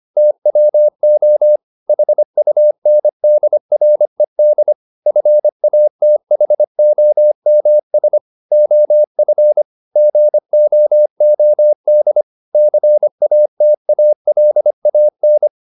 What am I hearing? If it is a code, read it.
Morse code, heard as a keyed tone, about 25 wpm, 600 Hz. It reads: TWO HUNDRED FATHOMS OF GOOD CATALAN
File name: the old man and the sea part 1 25wpm